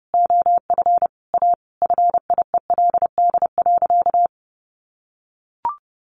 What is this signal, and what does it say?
Morse code, heard as a keyed tone, about 30 wpm, 700 Hz. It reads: OF A FIELD.
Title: Sentences from Top 1000 Words 30wpm